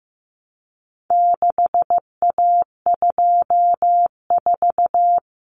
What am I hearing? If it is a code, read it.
Morse code, heard as a keyed tone, about 15 wpm, 700 Hz. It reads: 6A24
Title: Common Cut Numbers - 4-Digit 15wpm